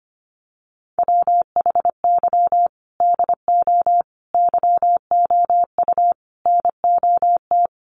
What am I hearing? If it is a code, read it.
Morse code, heard as a keyed tone, about 25 wpm, 700 Hz. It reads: WHY DO YOU NOT
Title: Sentences from Top 1400 Words 25wpm